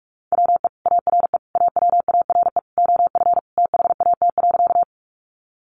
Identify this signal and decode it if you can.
Morse code, heard as a keyed tone, about 44 wpm, 700 Hz. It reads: WE ARE AWARE OF THAT.
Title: Sentences from Top 1200 Words 45wpm